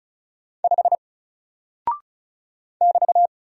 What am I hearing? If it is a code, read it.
Morse code, heard as a keyed tone, about 36 wpm, 700 Hz. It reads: 5 <BT>
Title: Single Character (Letter:Number:Punctuation:Prosign) Alt 35wpm